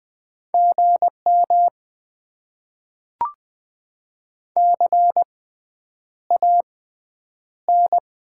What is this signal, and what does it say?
Morse code, heard as a keyed tone, about 20 wpm, spaced as for 9 wpm, 700 Hz. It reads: GM CAN
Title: Top 90 QSO Elements - 6x Character Spacing - 20wpm